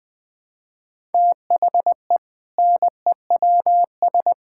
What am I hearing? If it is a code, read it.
Morse code, heard as a keyed tone, about 20 wpm, 700 Hz. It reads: THE NEWS
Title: Sentences with 5-Word Phrases - 20wpm